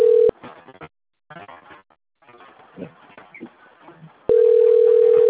Ringback tone (North America), 440 + 480 Hz.